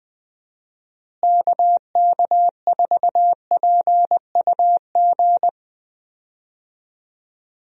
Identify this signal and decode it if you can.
Morse code, heard as a keyed tone, about 20 wpm, 700 Hz. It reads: KK4PUG
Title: USA Call Signs - 6 Characters - Repeated 2x - 20wpm